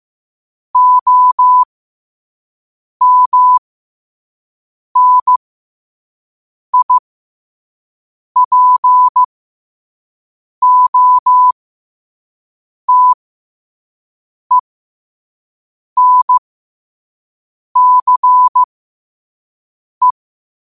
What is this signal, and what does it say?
Morse code, heard as a keyed tone, about 15 wpm, spaced as for 5 wpm, 1000 Hz. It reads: OMNIPOTENCE
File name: qotdfiles / 5wpm